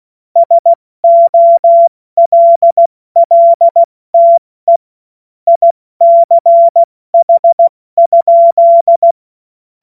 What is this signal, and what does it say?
Morse code, heard as a keyed tone, about 16 wpm, 680 Hz. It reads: SOLLTE ICH?
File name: Morsekurs / Witze / Witz 08. November 2022 70 F80